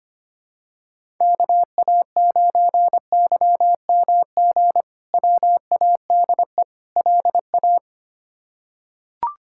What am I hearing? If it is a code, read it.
Morse code, heard as a keyed tone, about 25 wpm, 700 Hz. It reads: KA9YMG WADE LA
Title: K1USN SST QSO Practice - 25wpm